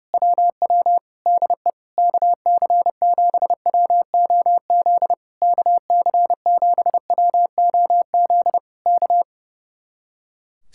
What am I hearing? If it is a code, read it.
Morse code, heard as a keyed tone, about 28 wpm, 700 Hz. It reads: WW DE KC7WOZ KC7WOZ K
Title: Speed-Racing Common QSO Phrases - 30wpm